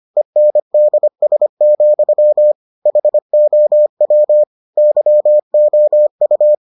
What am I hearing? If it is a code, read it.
Morse code, heard as a keyed tone, about 25 wpm, 600 Hz. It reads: ENDS, HOW YOU